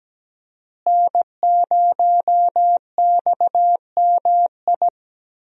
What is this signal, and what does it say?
Morse code, heard as a keyed tone, about 17 wpm, 700 Hz. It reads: N0XMI
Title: USA Call Signs - 5 Characters - Repeated 2x - 17wpm